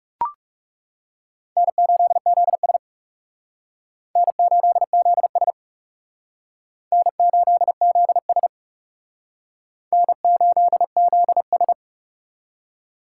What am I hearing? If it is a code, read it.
Morse code, heard as a keyed tone, about 37 wpm, 700 Hz. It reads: N8ZS N8ZS N8ZS N8ZS
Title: Speed Racing - USA Call Signs 30wpm